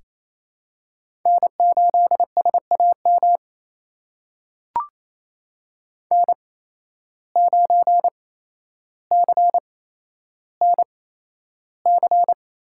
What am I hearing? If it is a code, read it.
Morse code, heard as a keyed tone, about 27 wpm, spaced as for 12 wpm, 700 Hz. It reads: N8SAM N9CNC
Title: USA Call Signs - 5 Characters - 8x Character Spacing - 28wpm